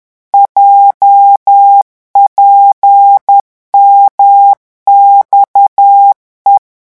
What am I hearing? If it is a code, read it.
Morse code, heard as a keyed tone, about 10 wpm, 800 Hz. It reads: JPMXE
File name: morse mp3 / revision11 lettres vitesse 10